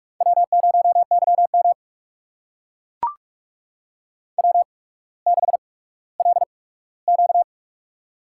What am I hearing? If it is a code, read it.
Morse code, heard as a keyed tone, about 44 wpm, spaced as for 19 wpm, 690 Hz. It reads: W0YM W6LQ
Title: USA Call Signs - 4 Characters - 8x Character Spacing - 45wpm